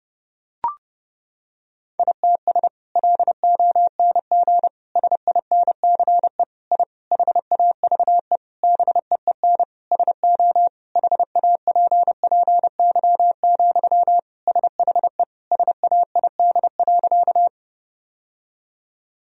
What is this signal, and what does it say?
Morse code, heard as a keyed tone, about 30 wpm, 700 Hz. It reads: ITS LONG SINCE I HAVE BEEN SO HAPPY, SHE SAID.
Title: Sentences from Top 1000 Words - Review Part 1 - 30wpm